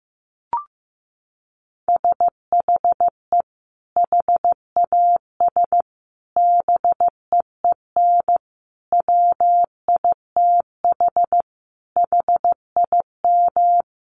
Morse code, heard as a keyed tone, about 15 wpm, 700 Hz. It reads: SHE HAS BEEN WITH HIM
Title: Sentences from Top 200 Words 15wpm